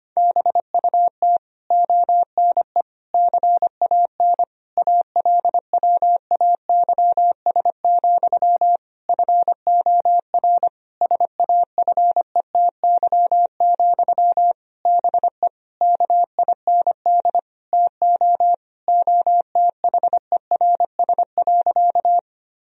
Morse code, heard as a keyed tone, about 25 wpm, 700 Hz. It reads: BUT ONE CAN ALWAYS, FOR SAFETY, BE KIND TO OTHERS.
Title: Sentences from Top 1000 Words - Review Part 4 - 25wpm